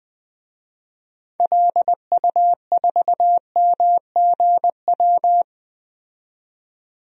Morse code, heard as a keyed tone, about 21 wpm, 700 Hz. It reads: LU4MGW
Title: DXCC-100 Call Signs - 6 Characters - 20wpm